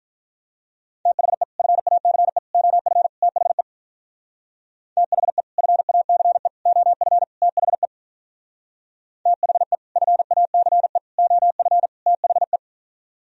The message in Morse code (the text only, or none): THE FACE OF THE THE FACE OF THE THE FACE OF THE